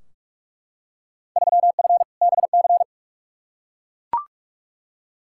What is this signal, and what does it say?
Morse code, heard as a keyed tone, about 41 wpm, 700 Hz. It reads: 3F BC
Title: Field Day Exchanges Encore - 45wpm